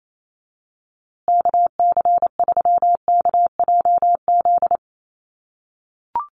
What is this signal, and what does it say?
Morse code, heard as a keyed tone, about 28 wpm, 700 Hz. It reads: KC3KJZ
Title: USA Call Signs Repeated 2x 28wpm